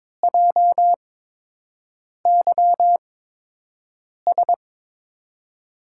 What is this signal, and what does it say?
Morse code, heard as a keyed tone, about 22 wpm, spaced as for 6 wpm, 700 Hz. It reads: JYS
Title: USA Call Signs - 6 Characters - 8x Character Spacing - 22wpm